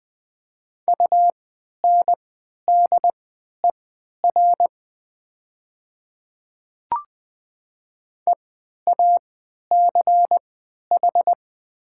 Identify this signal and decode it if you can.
Morse code, heard as a keyed tone, about 20 wpm, spaced as for 11 wpm, 700 Hz. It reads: UNDER EACH
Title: 3x Character Spacing - Top 200 Words - 20wpm